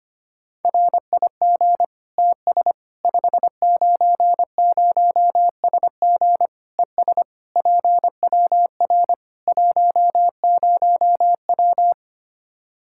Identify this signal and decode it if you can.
Morse code, heard as a keyed tone, about 25 wpm, 700 Hz. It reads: RIG TS 590SG ES PWR 10W
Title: Practice QSO - Exchange 2 - Info - 25wpm